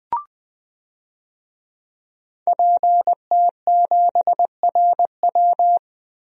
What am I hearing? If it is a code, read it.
Morse code, heard as a keyed tone, about 21 wpm, 700 Hz. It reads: PT7RW
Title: DXCC-100 Call Signs - 5 Characters - 20wpm